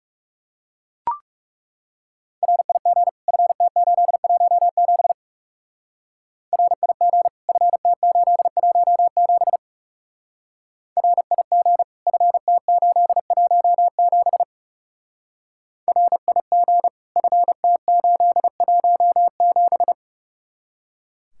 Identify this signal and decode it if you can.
Morse code, heard as a keyed tone, about 37 wpm, 700 Hz. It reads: RIG FT817 RIG FT817 RIG FT817 RIG FT817
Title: Speed-Racing Common QSO Phrases - 30wpm